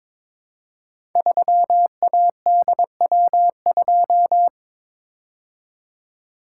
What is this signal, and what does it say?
Morse code, heard as a keyed tone, about 22 wpm, 700 Hz. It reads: 3ADW2
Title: Code Groups - 5 Letter:Numbers - 22wpm